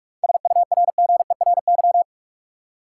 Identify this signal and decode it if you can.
Morse code, heard as a keyed tone, about 46 wpm, 700 Hz. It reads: SURGERY
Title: Seven Letter Words - 45wpm